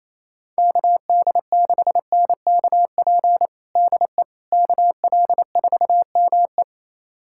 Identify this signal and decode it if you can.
Morse code, heard as a keyed tone, about 28 wpm, 700 Hz. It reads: KD6NKP DE KL4ME
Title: USA Callsign DE Callsign 28wpm